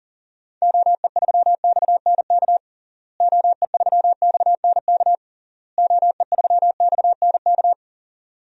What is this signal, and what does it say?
Morse code, heard as a keyed tone, about 40 wpm, 700 Hz. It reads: OE3XNK OE3XNK OE3XNK
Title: DXCC-100 Call Signs - 6 Characters - Repeated 3x - 40wpm